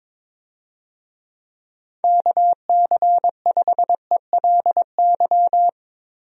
Morse code, heard as a keyed tone, about 22 wpm, 700 Hz. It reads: KC5ELY
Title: USA Call Signs - 6 Characters - Repeated 2x - 22wpm